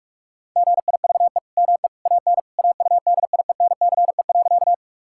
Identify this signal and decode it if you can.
Morse code, heard as a keyed tone, about 46 wpm, 700 Hz. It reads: GIVE ME AN AUDIENCE.
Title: Sentences from Top 1000 Words - Review Part 5 - 45wpm